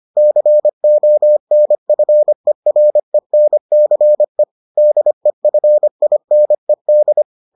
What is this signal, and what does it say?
Morse code, heard as a keyed tone, about 25 wpm, 600 Hz. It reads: CONFERENCE DEFINED